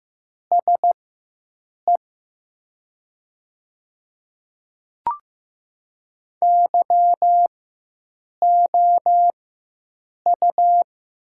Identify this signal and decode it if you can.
Morse code, heard as a keyed tone, about 15 wpm, spaced as for 7 wpm, 700 Hz. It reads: SE YOU
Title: 4x Character Spacing - Top 100 Words - 15wpm